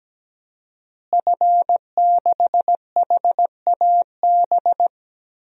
Morse code, heard as a keyed tone, about 17 wpm, 700 Hz. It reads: F6HAB
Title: DXCC-100 Call Signs - 5 Characters - Repeated 3x - 17wpm